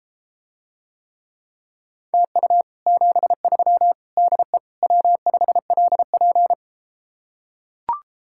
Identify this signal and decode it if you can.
Morse code, heard as a keyed tone, about 33 wpm, 700 Hz. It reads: TU 73 DE W5LP